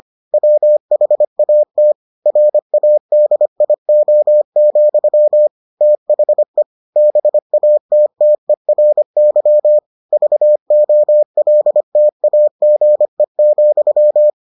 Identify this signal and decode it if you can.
Morse code, heard as a keyed tone, about 25 wpm, 600 Hz. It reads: WHAT RADIO, THE BATTERY VOLTAGE,